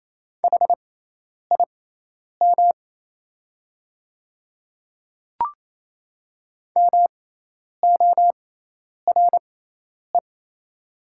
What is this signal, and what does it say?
Morse code, heard as a keyed tone, about 27 wpm, spaced as for 10 wpm, 700 Hz. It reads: HIM MORE
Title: 6x Character Spacing - Top 100 Words - 28wpm